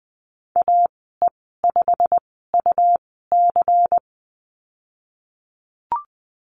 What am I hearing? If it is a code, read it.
Morse code, heard as a keyed tone, about 20 wpm, spaced as for 15 wpm, 700 Hz. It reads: AE5UC
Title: 147 Lesson 24g - Letter y - Call Signs 20wpm